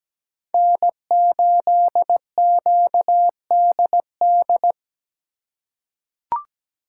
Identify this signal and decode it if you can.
Morse code, heard as a keyed tone, about 17 wpm, 700 Hz. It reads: N8QDD
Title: USA Call Signs - 3x Character Spacing - 17wpm